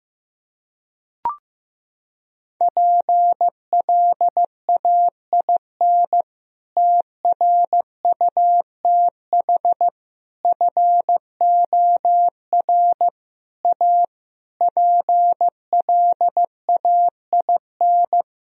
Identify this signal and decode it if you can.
Morse code, heard as a keyed tone, about 15 wpm, 700 Hz. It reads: PLAIN TRUTH FOR A PLAIN
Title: Sentences from Top 1000 Words - Review Part 2 - 15wpm